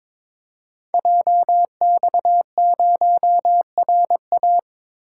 Morse code, heard as a keyed tone, about 22 wpm, 700 Hz. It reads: JX0RA
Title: Code Groups - 5 Letter:Numbers - 22wpm